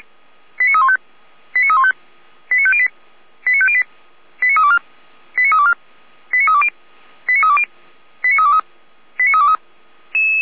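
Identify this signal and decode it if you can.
Five-tone selective call (ZVEI), heard as quick runs of five tones, tones of about 70 ms. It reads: 89315 89315 89508 89508 89324 89324 89320 89320 89323 89323
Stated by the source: ZVEI five-tone selective call, address 89322, received on an SDR.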